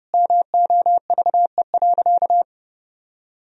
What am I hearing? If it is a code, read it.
Morse code, heard as a keyed tone, about 30 wpm, 700 Hz. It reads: MOVE.
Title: Sentences from Top 1000 Words - Review Part 3 - 30wpm